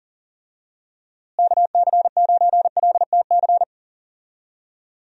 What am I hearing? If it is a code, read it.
Morse code, heard as a keyed tone, about 39 wpm, 700 Hz. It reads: KC9LTC
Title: USA Call Signs - 6 Characters - Repeated 2x - 40wpm